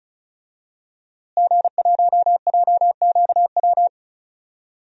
Morse code, heard as a keyed tone, about 36 wpm, 700 Hz. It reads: G1JQW
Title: DXCC-100 Call Signs - 5 Characters - Repeated 3x - 35wpm